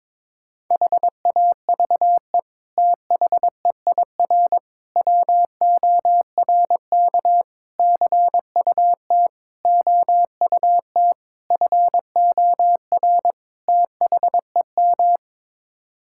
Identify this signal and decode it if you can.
Morse code, heard as a keyed tone, about 22 wpm, 700 Hz. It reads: HAVE THEIR WORK CUT OUT FOR THEM
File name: practice / 22wpm / Sets of 7 Words - 22wpm